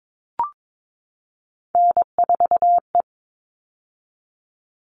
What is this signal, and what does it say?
Morse code, heard as a keyed tone, about 22 wpm, 700 Hz. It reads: N4E
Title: All Cut Numbers - 3-Digit 22wpm